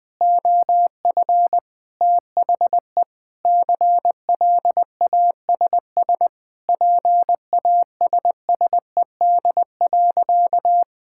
Morse code, heard as a keyed tone, about 20 wpm, 700 Hz. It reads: OF THE CLASS PASSED.
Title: Sentences with 5-Word Phrases - 20wpm